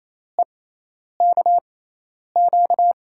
Morse code, heard as a keyed tone, about 28 wpm, spaced as for 10 wpm, 700 Hz. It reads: EKQ
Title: USA Call Signs - 6 Characters - 6x Character Spacing - 28wpm